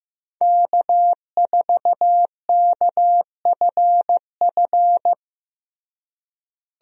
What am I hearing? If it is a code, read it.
Morse code, heard as a keyed tone, about 15 wpm, 700 Hz. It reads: K4KFF
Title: USA Call Signs - 5 Characters - Repeated 3x - 15wpm